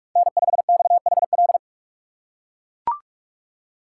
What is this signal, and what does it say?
Morse code, heard as a keyed tone, about 42 wpm, 700 Hz. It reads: N5XHL
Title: USA Call Signs - 5 Characters - 4x Character Spacing - 45wpm